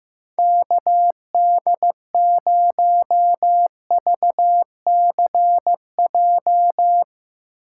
Morse code, heard as a keyed tone, about 17 wpm, 700 Hz. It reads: KD0VCJ
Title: Speed Racing - USA 2x3 Call Signs 15wpm